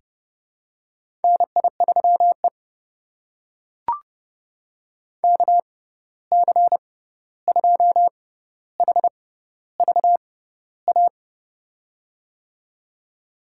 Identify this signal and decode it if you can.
Morse code, heard as a keyed tone, about 30 wpm, spaced as for 14 wpm, 700 Hz. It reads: NI3E KC2HVA
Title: USA Call Signs - 6x Character Spacing - 30wpm